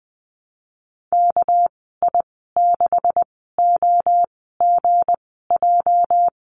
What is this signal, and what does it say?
Morse code, heard as a keyed tone, about 19 wpm, spaced as for 14 wpm, 700 Hz. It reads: KI6OGJ
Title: 213 Lesson 34g - Number 8 - Call Signs 20wpm